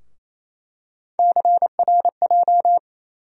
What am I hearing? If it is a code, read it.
Morse code, heard as a keyed tone, about 28 wpm, 700 Hz. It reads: CRJ
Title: USA Call Sign Postfixes - 3 Letters 28wpm